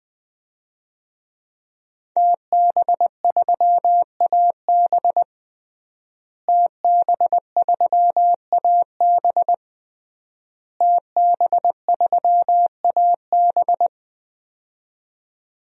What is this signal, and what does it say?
Morse code, heard as a keyed tone, about 20 wpm, 700 Hz. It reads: TB3AB TB3AB TB3AB
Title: DXCC-100 Call Signs - 5 Characters - Repeated 3x - 20wpm